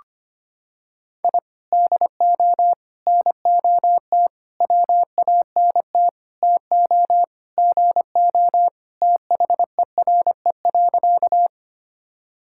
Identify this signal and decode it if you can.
Morse code, heard as a keyed tone, about 25 wpm, 700 Hz. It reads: I DO NOT WANT TO GO THERE.
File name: practice / 25wpm / Sentences with 4-Word Phrases - 25wpm